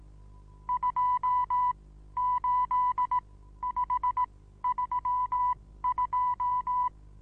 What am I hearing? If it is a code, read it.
Morse code, heard as a keyed tone, about 18 wpm, spaced as for 13 wpm, 1000 Hz. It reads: 28532
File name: Onlinelehrgang / morsen / DJ4UF Morsen25